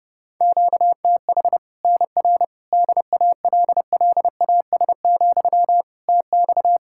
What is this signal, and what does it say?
Morse code, heard as a keyed tone, about 30 wpm, 700 Hz. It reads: QTH NR DALLAS, TX
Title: Practice QSO - Exchange 1 - Beginning - 30wpm